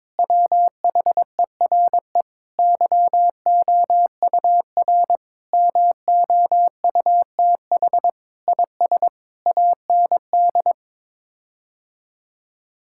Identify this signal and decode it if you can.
Morse code, heard as a keyed tone, about 22 wpm, 700 Hz. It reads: WHERE YOUR MOUTH IS AND
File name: practice / 22wpm / Sets of 8 Words - 22wpm